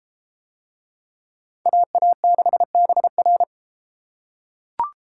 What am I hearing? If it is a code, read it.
Morse code, heard as a keyed tone, about 30 wpm, 700 Hz. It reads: AA6BR